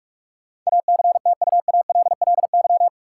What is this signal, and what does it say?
Morse code, heard as a keyed tone, about 46 wpm, 700 Hz. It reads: ACTUALLY
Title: Sentences from Top 700 Words - New Word Drill - 45wpm